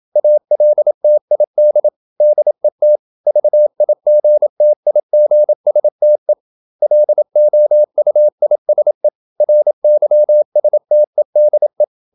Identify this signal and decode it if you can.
Morse code, heard as a keyed tone, about 27 wpm, 600 Hz. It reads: ALTID DET VIGTIGSTE LOUISE RYSTEDE
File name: Ferie QRQ 27wpm